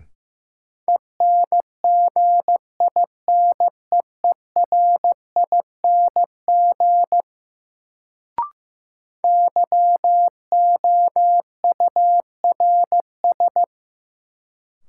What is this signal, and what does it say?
Morse code, heard as a keyed tone, about 15 wpm, 700 Hz. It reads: ENGINEERING YOURS
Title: Sentences from Top 1500 Words - New Word Drill Alt - 15wpm